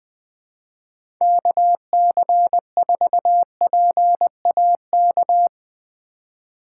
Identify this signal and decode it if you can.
Morse code, heard as a keyed tone, about 20 wpm, 700 Hz. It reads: KC4PAK
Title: USA Call Signs - 6 Characters - 4x Character Spacing - 20wpm